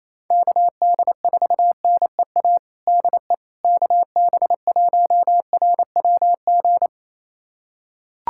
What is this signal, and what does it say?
Morse code, heard as a keyed tone, about 28 wpm, 700 Hz. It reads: KD4NEA DE KB1RWG
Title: USA Callsign DE Callsign - 6 Characters - 28wpm